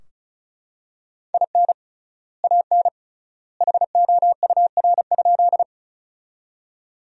Morse code, heard as a keyed tone, about 36 wpm, 700 Hz. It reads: IN AN HOUR?
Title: 3x Word-Spacing - Sentences from Top 400 Words - 35wpm